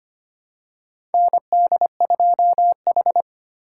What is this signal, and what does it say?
Morse code, heard as a keyed tone, about 25 wpm, 700 Hz. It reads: ND2H
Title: USA Call Signs - 3x Character Spacing - 25wpm